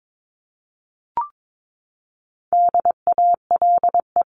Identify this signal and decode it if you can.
Morse code, heard as a keyed tone, about 22 wpm, 700 Hz. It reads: DALE
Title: CWT Exchange 22wpm